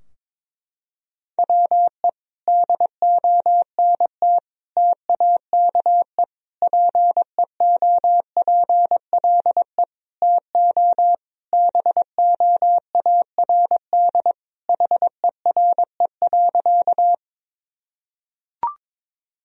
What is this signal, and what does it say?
Morse code, heard as a keyed tone, about 22 wpm, 700 Hz. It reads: WE DONT TAKE PEOPLE TO BOARD HERE.
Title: Sentences from Top 1000 Words - Review Part 4 - 22wpm